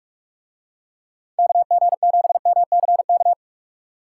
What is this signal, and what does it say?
Morse code, heard as a keyed tone, about 47 wpm, 700 Hz. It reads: KG7MCK